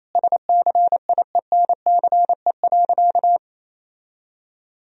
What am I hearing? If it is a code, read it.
Morse code, heard as a keyed tone, about 28 wpm, 700 Hz. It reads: SCIENCE.
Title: Sentences from Top 1000 Words - Review Part 2 - 28wpm